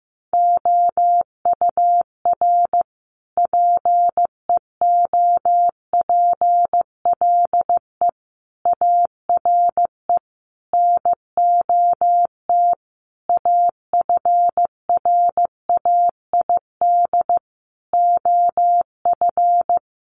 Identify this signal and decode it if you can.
Morse code, heard as a keyed tone, about 15 wpm, 700 Hz. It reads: OUR PEOPLE ARE NOT AFRAID OF
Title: Sentences from Top 1500 Words 15wpm